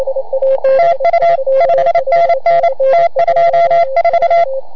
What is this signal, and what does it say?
Morse code, heard as a keyed tone, about 27 wpm, 690 Hz. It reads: TU 5NN T24